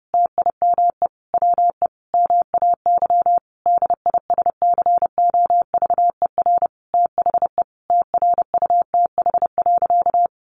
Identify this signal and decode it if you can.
Morse code, heard as a keyed tone, about 30 wpm, 700 Hz. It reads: TIME WE MAY DISCOVER THE TRUTH.
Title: Sentences from Top 1100 Words 30wpm